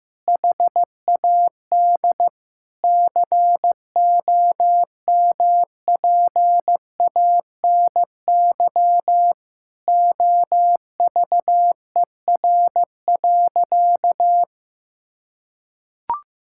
Morse code, heard as a keyed tone, about 15 wpm, 700 Hz. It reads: HAD COMPANY OVER.
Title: Sentences from Top 1000 Words - Review Part 6 - 15wpm